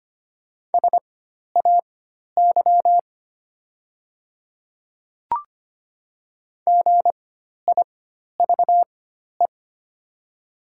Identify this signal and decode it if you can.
Morse code, heard as a keyed tone, about 25 wpm, spaced as for 12 wpm, 700 Hz. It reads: SAY GIVE
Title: 4x Character Spacing - Top 200 Words - 25wpm